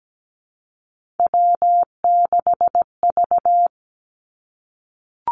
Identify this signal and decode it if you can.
Morse code, heard as a keyed tone, about 17 wpm, 700 Hz. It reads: W6V
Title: Three Challenging Letter-Number Combinations 17wpm